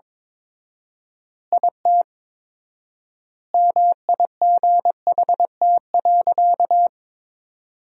Morse code, heard as a keyed tone, about 22 wpm, 700 Hz. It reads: IT MIGHT.
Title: 4x Word-Spacing - Sentences from Top 500 Words - 22wpm